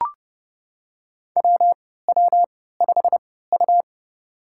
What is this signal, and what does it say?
Morse code, heard as a keyed tone, about 32 wpm, spaced as for 17 wpm, 700 Hz. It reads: WW5U
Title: USA Call Signs - 4 Characters - 3x Character Spacing - 30wpm